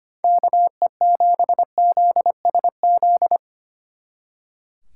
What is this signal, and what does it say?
Morse code, heard as a keyed tone, about 33 wpm, 700 Hz. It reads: KE7ZSZ